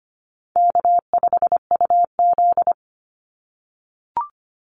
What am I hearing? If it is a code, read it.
Morse code, heard as a keyed tone, about 25 wpm, 700 Hz. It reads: K5UZ
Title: Easy Call Signs - 25wpm